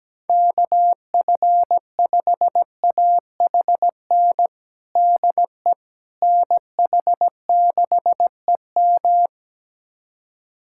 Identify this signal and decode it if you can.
Morse code, heard as a keyed tone, about 17 wpm, 700 Hz. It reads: KF5AHN DE NH6EM
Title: USA Callsign DE Callsign 17wpm